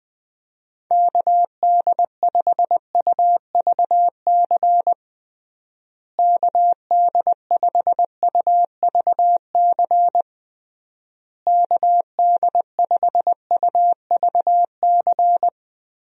Morse code, heard as a keyed tone, about 20 wpm, 700 Hz. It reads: KD5UVC KD5UVC KD5UVC
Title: USA Call Signs - 6 Characters - Repeated 3x - 20wpm